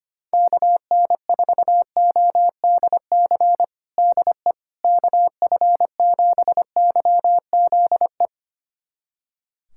Morse code, heard as a keyed tone, about 25 wpm, 700 Hz. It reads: KN4ODC DE KF7YZE
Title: USA Callsign DE Callsign 25wpm